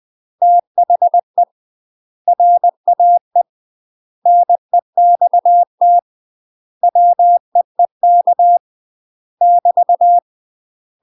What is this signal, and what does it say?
Morse code, heard as a keyed tone, about 20 wpm, 700 Hz. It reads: THE RAE NEXT WEEK <BT>
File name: UK MASTER QSOs TRAINING / 20WPM / UK MASTER QSOs 0071